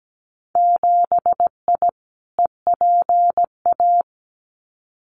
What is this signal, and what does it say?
Morse code, heard as a keyed tone, about 17 wpm, 700 Hz. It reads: 7I EPA
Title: Winter Field Day Exchanges Encore 17wpm